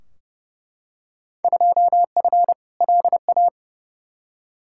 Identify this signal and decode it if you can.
Morse code, heard as a keyed tone, about 30 wpm, 700 Hz. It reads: 2F LA
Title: Field Day Exchanges - 30wpm